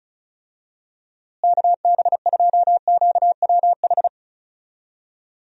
Morse code, heard as a keyed tone, about 36 wpm, 700 Hz. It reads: KB2QWH
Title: USA Call Signs - 6 Characters - Repeated 3x - 35wpm